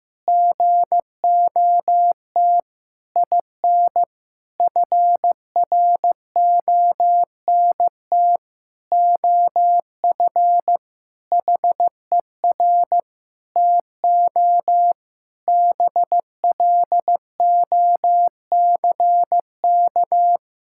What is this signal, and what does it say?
Morse code, heard as a keyed tone, about 15 wpm, 700 Hz. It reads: GOT IN FRONT OF HER TO BLOCK
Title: Sentences from Top 1600 Words 15wpm